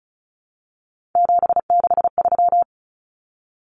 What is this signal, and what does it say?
Morse code, heard as a keyed tone, about 34 wpm, 700 Hz. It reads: 763
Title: Numbers - 3-Digit 35wpm